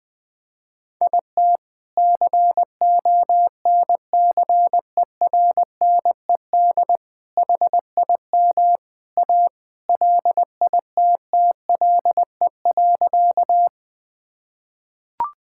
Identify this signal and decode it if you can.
Morse code, heard as a keyed tone, about 20 wpm, 700 Hz. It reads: IT CONCERNED HIM A LITTLE.
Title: Sentences from Top 1000 Words - Review Part 6 - 20wpm